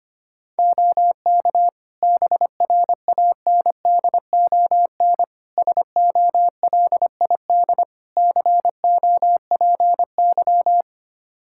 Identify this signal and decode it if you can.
Morse code, heard as a keyed tone, about 25 wpm, 700 Hz. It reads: OK BRANDON SOLID COPY
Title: K1USN SST QSO Practice 25wpm